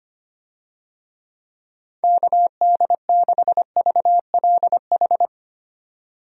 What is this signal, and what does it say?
Morse code, heard as a keyed tone, about 25 wpm, 700 Hz. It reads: KD6VLH
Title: USA Call Signs - 6 Characters - 25wpm